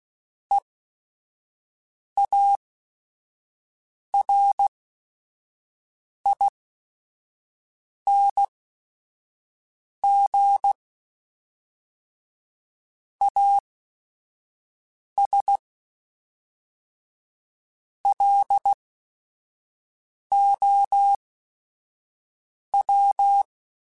Morse code, heard as a keyed tone, about 16 wpm, spaced as for 5 wpm, 800 Hz. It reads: EARING AS LOW
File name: Technical Subjects / Technical Subjects-mp3s / XC @5